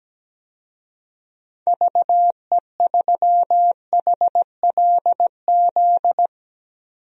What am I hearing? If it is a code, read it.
Morse code, heard as a keyed tone, about 17 wpm, 700 Hz. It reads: VE3HLZ
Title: DXCC-100 Call Signs - 6 Characters - 17wpm